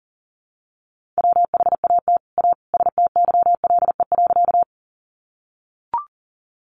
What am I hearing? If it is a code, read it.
Morse code, heard as a keyed tone, about 40 wpm, 690 Hz. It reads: WHAT A STYLE.
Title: Sentences from Top 800 Words 40wpm